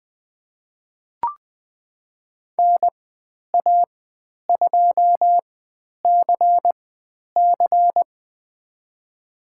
Morse code, heard as a keyed tone, about 20 wpm, spaced as for 10 wpm, 700 Hz. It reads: NA2CC
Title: 109 Lesson 18g - Letter m - Call Signs 20wpm